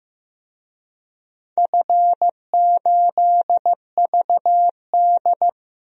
Morse code, heard as a keyed tone, about 15 wpm, 700 Hz. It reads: F8VD
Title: DXCC-100 Call Signs - 4 Characters - 15wpm